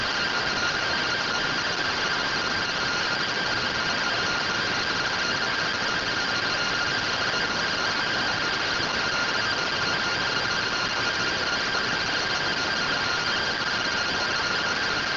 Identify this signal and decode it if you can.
Two-tone teleprinter signal, about 170 Hz shift.
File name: findings / MR AXIAL fr1 120.gen-10 rtty rx